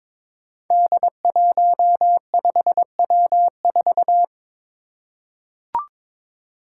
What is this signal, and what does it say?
Morse code, heard as a keyed tone, about 22 wpm, 700 Hz. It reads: D15W4
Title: Five Challenging Letter-Number Combinations 22wpm